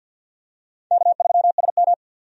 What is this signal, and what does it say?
Morse code, heard as a keyed tone, about 51 wpm, 700 Hz. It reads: K3SM
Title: USA Call Signs - 4 Characters - 50wpm